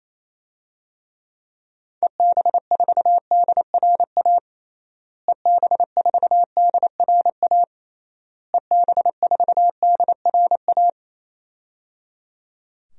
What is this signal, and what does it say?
Morse code, heard as a keyed tone, about 28 wpm, 700 Hz. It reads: EB4DRA EB4DRA EB4DRA
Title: DXCC-100 Call Signs - 6 Characters - Repeated 3x - 28wpm